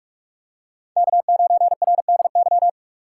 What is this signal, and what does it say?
Morse code, heard as a keyed tone, about 42 wpm, 700 Hz. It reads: K9RDY
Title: USA Call Signs - 2x Character Spacing - 45wpm